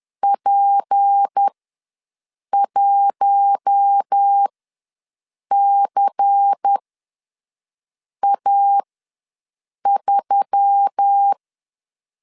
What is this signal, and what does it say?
Morse code, heard as a keyed tone, about 10 wpm, spaced as for 6 wpm, 800 Hz. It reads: P1CA3
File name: LEZIONI / 7 GF9Z / 7 G ws 10wpm 3843